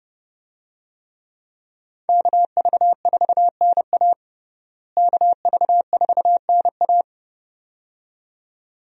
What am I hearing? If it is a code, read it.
Morse code, heard as a keyed tone, about 32 wpm, 700 Hz. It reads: KV4NA KV4NA
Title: USA Call Signs - 5 Characters - Repeated 2x - 30wpm